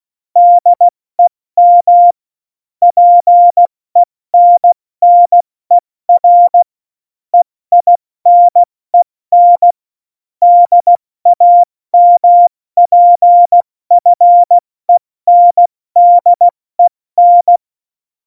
Morse code, heard as a keyed tone, about 16 wpm, 700 Hz. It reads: DEM PENNER EINEN DAMPFENDEN
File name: Morsekurs / Witze / Witz 07. Januar 2020, 70 F80